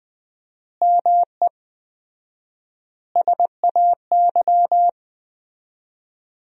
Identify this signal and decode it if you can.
Morse code, heard as a keyed tone, about 20 wpm, 700 Hz. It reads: ME SAY
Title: 4x Word-Spacing - Sentences from Top 500 Words - 20wpm